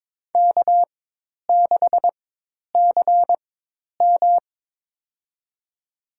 Morse code, heard as a keyed tone, about 21 wpm, spaced as for 10 wpm, 700 Hz. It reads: K6CM
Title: USA Call Signs - 4x Character Spacing - 22wpm